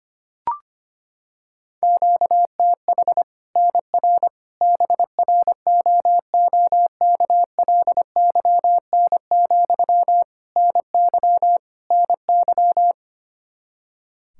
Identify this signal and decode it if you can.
Morse code, heard as a keyed tone, about 25 wpm, 700 Hz. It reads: QTH NR BROOKLYN, NY NY
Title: Practice QSO - Exchange 1 - Beginning - 25wpm